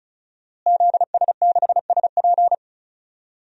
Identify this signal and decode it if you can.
Morse code, heard as a keyed tone, about 34 wpm, 700 Hz. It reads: ZS6SP
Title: DXCC-100 Call Signs - 5 Characters - Repeated 3x - 35wpm